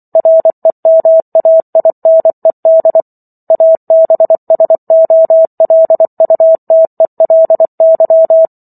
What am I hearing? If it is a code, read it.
Morse code, heard as a keyed tone, about 24 wpm, 650 Hz. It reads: REMAINED ABSOLUTELY
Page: War of the Worlds - 15-Chapter 15 - 24 WPM